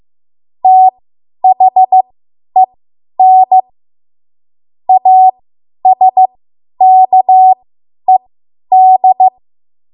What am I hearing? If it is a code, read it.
Morse code, heard as a keyed tone, about 15 wpm, spaced as for 10 wpm, 750 Hz. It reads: THEN ASKED
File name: Morse / 10WPM